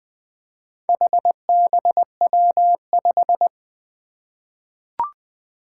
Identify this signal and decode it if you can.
Morse code, heard as a keyed tone, about 20 wpm, 700 Hz. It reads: HBW5
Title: Four Challenging Letter-Number Combinations 20wpm